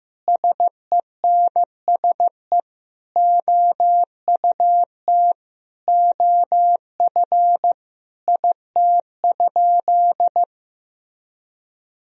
Morse code, heard as a keyed tone, about 16 wpm, 700 Hz. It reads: SENSE OUT OF IT?
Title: Sentences from Top 1000 Words - Review Part 4 - 15wpm